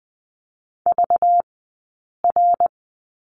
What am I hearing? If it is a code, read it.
Morse code, heard as a keyed tone, about 20 wpm, spaced as for 8 wpm, 700 Hz. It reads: VR
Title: 197 Lesson 32d - Letter k - Two Characters 20wpm